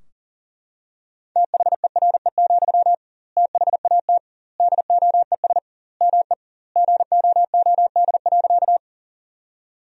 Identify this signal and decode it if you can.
Morse code, heard as a keyed tone, about 40 wpm, 700 Hz. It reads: THERE, THAT DOES ME GOOD.
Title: 2x Word-Spacing - Sentences from Top 200 Words - 40wpm